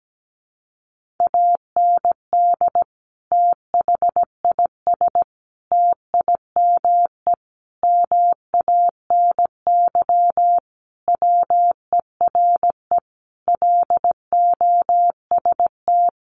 Morse code, heard as a keyed tone, about 17 wpm, 700 Hz. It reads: AND THIS TIME MANY WERE LOST
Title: Sentences from Top 1400 Words 17wpm